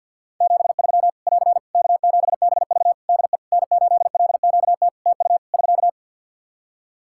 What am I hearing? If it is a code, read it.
Morse code, heard as a keyed tone, about 50 wpm, 700 Hz. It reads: 73 <AR> K7BV DE N8LQT TU <SK>
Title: Practice QSO - Exchange 4 - Ending - 50wpm